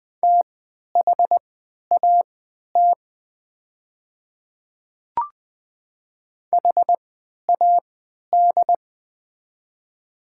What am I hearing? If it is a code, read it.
Morse code, heard as a keyed tone, about 20 wpm, spaced as for 11 wpm, 700 Hz. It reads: THAT HAD
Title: 3x Character Spacing - Top 100 Words - 20wpm